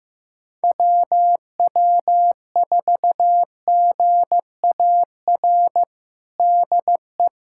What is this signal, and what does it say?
Morse code, heard as a keyed tone, about 15 wpm, 700 Hz. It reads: WW4GAR DE
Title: USA Callsign DE Callsign - 6 Characters - 15wpm